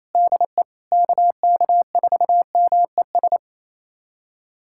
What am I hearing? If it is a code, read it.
Morse code, heard as a keyed tone, about 28 wpm, 700 Hz. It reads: DE KK4MES
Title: USA Callsign DE Callsign - 6 Characters - 28wpm